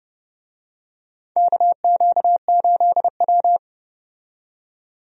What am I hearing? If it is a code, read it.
Morse code, heard as a keyed tone, about 30 wpm, 700 Hz. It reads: KQ8W
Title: USA Call Signs - 4 Characters - 30wpm